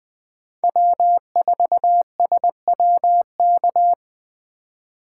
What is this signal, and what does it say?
Morse code, heard as a keyed tone, about 20 wpm, 700 Hz. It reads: W4SWK
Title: USA Call Signs - 5 Characters - 2x Character Spacing - 20wpm